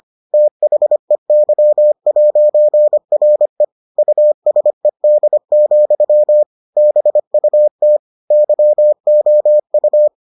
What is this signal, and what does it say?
Morse code, heard as a keyed tone, about 25 wpm, 600 Hz. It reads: THEY'RE USED, BUT YOU